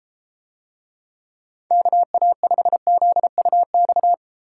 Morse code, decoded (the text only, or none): KA5ZUX